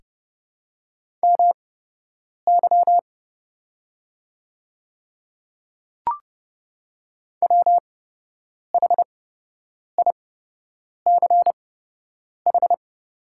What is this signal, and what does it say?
Morse code, heard as a keyed tone, about 29 wpm, spaced as for 8 wpm, 700 Hz. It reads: MY WHICH
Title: 8x Character Spacing - Top 100 Words - 30wpm